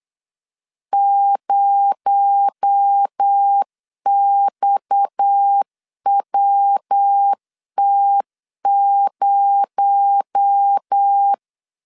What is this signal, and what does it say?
Morse code, heard as a keyed tone, about 8 wpm, 800 Hz. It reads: 0XWT0
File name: LEZIONI / 6 PKX0 / 6 PKX0 8wpm 3341